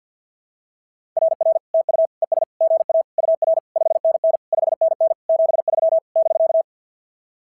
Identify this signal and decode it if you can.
Morse code, heard as a keyed tone, about 46 wpm, 660 Hz. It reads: RR TU ES GA UR 5NN 5NN 73 <BK>
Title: SOTA QSO Practice 50wpm